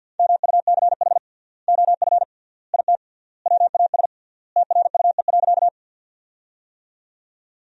Morse code, heard as a keyed tone, about 47 wpm, 700 Hz. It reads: MUCH OF IT WAS TRUE.
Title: 3x Word-Spacing - Sentences from Top 400 Words - 50wpm